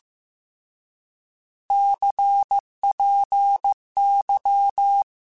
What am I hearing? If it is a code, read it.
Morse code, heard as a keyed tone, about 15 wpm, 800 Hz. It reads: CPY
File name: Misc / Misc-mp3s / Words @13